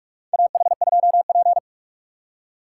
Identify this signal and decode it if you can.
Morse code, heard as a keyed tone, about 46 wpm, 700 Hz. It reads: AH2P
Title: USA Call Signs - 4 Characters - 6x Character Spacing - 45wpm